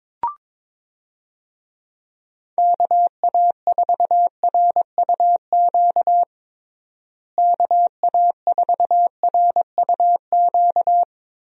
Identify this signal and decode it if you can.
Morse code, heard as a keyed tone, about 22 wpm, 700 Hz. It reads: KA4RUQ KA4RUQ
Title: USA Call Signs - 6 Characters - Repeated 2x - 22wpm